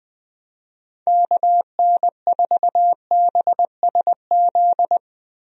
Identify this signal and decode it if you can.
Morse code, heard as a keyed tone, about 21 wpm, 700 Hz. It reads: KN4BSZ
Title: USA Call Signs - 6 Characters - 20wpm